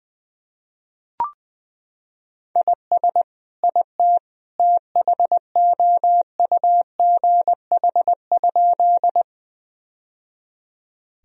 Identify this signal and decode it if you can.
Morse code, heard as a keyed tone, about 20 wpm, 700 Hz. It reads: IS IT THOUGH?
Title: Sentences from Top 1000 Words - Review Part 2 - 20wpm